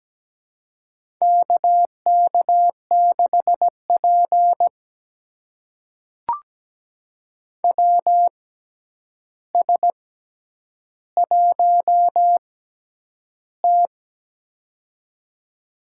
Morse code, heard as a keyed tone, about 17 wpm, spaced as for 9 wpm, 700 Hz. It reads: KK6P WS1T
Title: USA Call Signs - 4 Characters - 6x Character Spacing - 17wpm